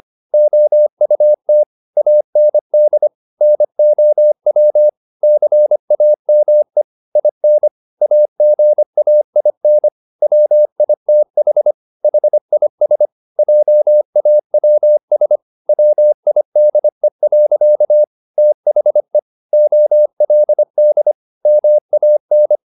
Morse code, heard as a keyed tone, about 25 wpm, 600 Hz. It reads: OUT AND NOW CAME IN AGAIN WITH HIS JAWS WIDE. THE OLD MAN